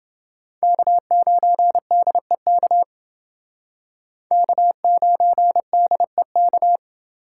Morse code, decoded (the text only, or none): K9DEK K9DEK